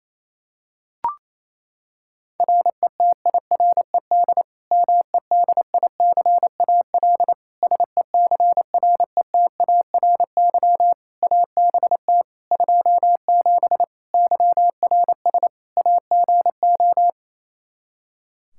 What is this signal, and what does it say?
Morse code, heard as a keyed tone, about 27 wpm, 700 Hz. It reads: RETIRED MEDICAL SECRETARY ABT 27 YRS AGO
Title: Practice QSO - Exchange 3 - More Info - 28wpm